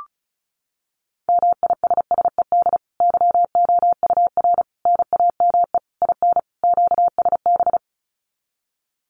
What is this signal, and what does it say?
Morse code, heard as a keyed tone, about 34 wpm, 700 Hz. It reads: MISSED YOUR NAME IN QSB
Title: Common QSO Phrases 35wpm